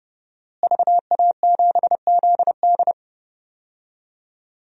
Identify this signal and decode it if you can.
Morse code, heard as a keyed tone, about 31 wpm, 700 Hz. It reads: VA7ZD
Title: DXCC-100 Call Signs - 5 Characters - Repeated 3x - 30wpm